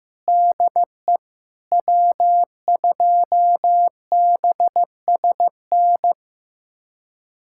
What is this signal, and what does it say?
Morse code, heard as a keyed tone, about 15 wpm, 700 Hz. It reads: DE W2BSN
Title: USA Callsign DE Callsign - 5 Characters - 15wpm